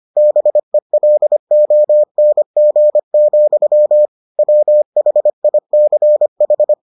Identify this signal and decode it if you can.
Morse code, heard as a keyed tone, about 25 wpm, 600 Hz. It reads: BELONG, WHICH